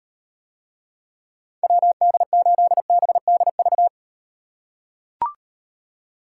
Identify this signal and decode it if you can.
Morse code, heard as a keyed tone, about 38 wpm, 700 Hz. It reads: WD8BDV